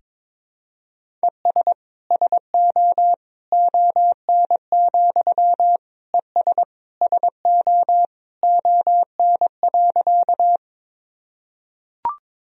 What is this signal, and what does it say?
Morse code, heard as a keyed tone, about 22 wpm, 700 Hz. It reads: ES SO ON, ES SO ON.
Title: Sentences from Top 100 Words and Abbreviations 22wpm